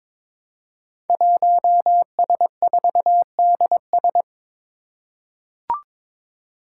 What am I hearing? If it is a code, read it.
Morse code, heard as a keyed tone, about 23 wpm, 700 Hz. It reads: 1S4DS
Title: Five Challenging Letter-Number Combinations 22wpm